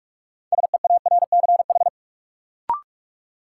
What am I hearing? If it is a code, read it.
Morse code, heard as a keyed tone, about 45 wpm, 700 Hz. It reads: SEARCH
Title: Sentences from Top 800 Words - New Word Drill - 45wpm